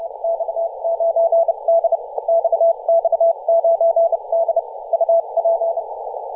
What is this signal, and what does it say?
Morse code, heard as a keyed tone, about 30 wpm, spaced as for 26 wpm, 670 Hz, with radio noise. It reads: X9D XX9D UP
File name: posts / 2017-feb-16-0552z-xx9d-21-026mhz-cw